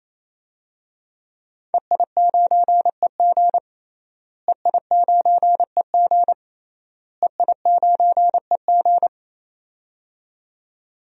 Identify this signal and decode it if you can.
Morse code, heard as a keyed tone, about 28 wpm, 700 Hz. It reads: EI9EG EI9EG EI9EG
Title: DXCC-100 Call Signs - 5 Characters - Repeated 3x - 28wpm